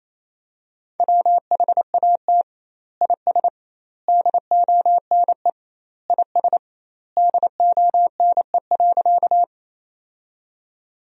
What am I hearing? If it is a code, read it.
Morse code, heard as a keyed tone, about 28 wpm, 700 Hz. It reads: WHAT IS DONE IS DONE.
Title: 2x Word-Spacing - Sentences from Top 500 Words - 28wpm